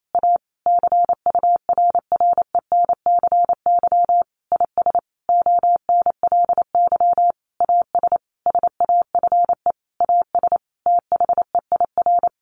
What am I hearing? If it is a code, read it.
Morse code, heard as a keyed tone, about 28 wpm, 700 Hz. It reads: A CURRENCY IS ONLY AS SAFE AS THEIR
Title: Sentences from Top 1500 Words 28wpm